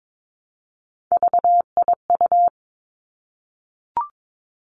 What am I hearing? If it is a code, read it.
Morse code, heard as a keyed tone, about 22 wpm, 700 Hz. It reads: VIU
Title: Three Challenging Letter-Number Combinations 22wpm